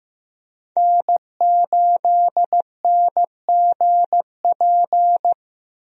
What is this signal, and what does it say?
Morse code, heard as a keyed tone, about 16 wpm, 700 Hz. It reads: N8NGP
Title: USA Call Signs - 5 Characters - Repeated 2x - 15wpm